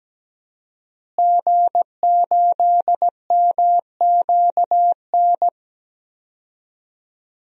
Morse code, heard as a keyed tone, about 17 wpm, 700 Hz. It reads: G8MQN
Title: DXCC-100 Call Signs - 5 Characters - Repeated 3x - 17wpm